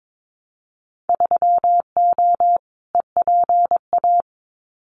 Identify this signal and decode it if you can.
Morse code, heard as a keyed tone, about 22 wpm, 700 Hz. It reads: 3O EPA
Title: Winter Field Day Exchanges 22wpm